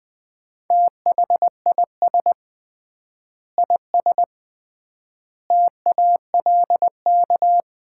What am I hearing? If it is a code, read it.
Morse code, heard as a keyed tone, about 20 wpm, 700 Hz. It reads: THIS IS TALK
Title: 3x Word-Spacing - Sets of 5 Words - 20wpm